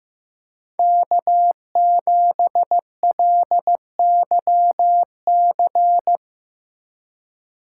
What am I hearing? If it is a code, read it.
Morse code, heard as a keyed tone, about 17 wpm, 700 Hz. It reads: K7LYC
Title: Speed Racing - USA 1x3 Call Signs 15wpm